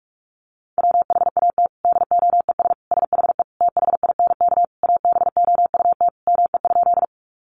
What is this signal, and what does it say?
Morse code, heard as a keyed tone, about 44 wpm, 700 Hz. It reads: WHAT DOES SHE THINK ABOUT ME?
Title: Sentences from Top 200 Words 45wpm